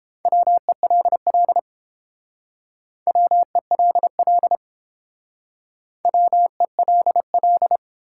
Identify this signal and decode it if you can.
Morse code, heard as a keyed tone, about 30 wpm, 700 Hz. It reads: WELL WELL WELL